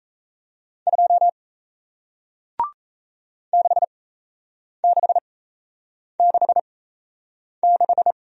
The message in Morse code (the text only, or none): 2 6666